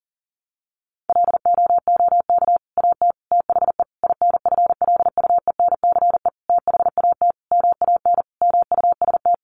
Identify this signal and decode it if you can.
Morse code, heard as a keyed tone, about 40 wpm, 700 Hz. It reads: LOOK AT THE INFLUENCE THAT MAN MUST
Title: Sentences from Top 1100 Words 40wpm